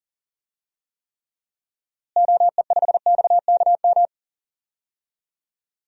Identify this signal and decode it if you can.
Morse code, heard as a keyed tone, about 40 wpm, 700 Hz. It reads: OE5XKM
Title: DXCC-100 Call Signs - 6 Characters - Repeated 3x - 40wpm